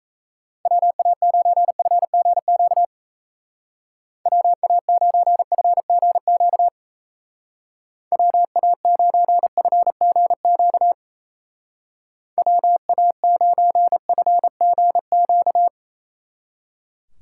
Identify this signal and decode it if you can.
Morse code, heard as a keyed tone, about 34 wpm, 700 Hz. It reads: WA9FGQ WA9FGQ WA9FGQ WA9FGQ
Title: Speed Racing - USA 2x3 Call Signs 28wpm